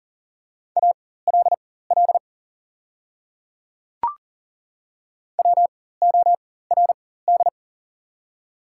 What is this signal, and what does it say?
Morse code, heard as a keyed tone, about 40 wpm, spaced as for 19 wpm, 700 Hz. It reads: ALL WORD
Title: 4x Character Spacing - Top 300 Words - 40wpm